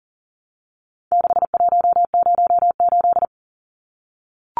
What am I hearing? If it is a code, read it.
Morse code, heard as a keyed tone, about 40 wpm, 700 Hz. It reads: 6108
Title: Numbers - 4-Digit 40wpm